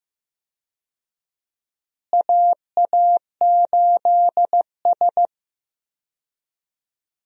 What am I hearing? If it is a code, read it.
Morse code, heard as a keyed tone, about 15 wpm, 700 Hz. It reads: AA8S
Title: USA Call Signs - 4 Characters - Repeated 3x - 15wpm